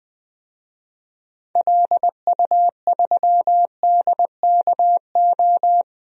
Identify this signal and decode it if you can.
Morse code, heard as a keyed tone, about 20 wpm, 700 Hz. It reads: LU3DKO
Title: DXCC-100 Call Signs - 6 Characters - 20wpm